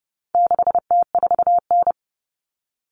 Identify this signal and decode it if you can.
Morse code, heard as a keyed tone, about 30 wpm, 700 Hz. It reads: 6T4N
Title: Common Cut Numbers - 4-Digit 30wpm